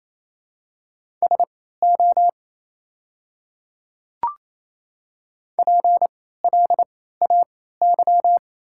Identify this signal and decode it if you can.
Morse code, heard as a keyed tone, about 28 wpm, spaced as for 16 wpm, 700 Hz. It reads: SO PLAY
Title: 3x Character Spacing - Top 200 Words - 28wpm